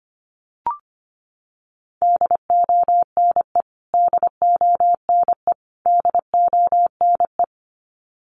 Morse code, heard as a keyed tone, about 25 wpm, 700 Hz. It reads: DONE DONE DONE
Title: Top 500 Words Repeated 25wpm